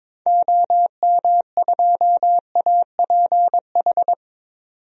Morse code, heard as a keyed tone, about 21 wpm, 700 Hz. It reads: OM2APH
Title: DXCC-100 Call Signs - 6 Characters - 22wpm